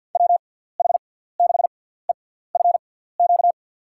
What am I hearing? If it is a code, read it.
Morse code, heard as a keyed tone, about 53 wpm, spaced as for 17 wpm, 700 Hz. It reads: WH6EFQ
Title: USA Call Signs - 6 Characters - 6x Character Spacing - 50wpm